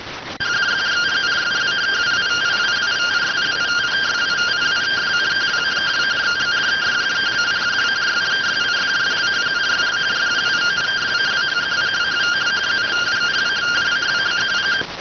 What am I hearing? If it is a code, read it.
Two-tone teleprinter signal, about 170 Hz shift.